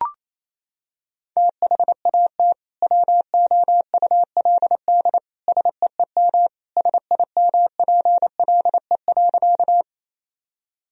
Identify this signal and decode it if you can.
Morse code, heard as a keyed tone, about 28 wpm, 700 Hz. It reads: THAT WOULD SEEM SIMPLE.
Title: Sentences from Top 1000 Words - Review Part 1 - 28wpm